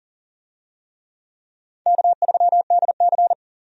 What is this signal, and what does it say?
Morse code, heard as a keyed tone, about 40 wpm, 700 Hz. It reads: K3DC
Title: USA Call Signs - 4 Characters - 40wpm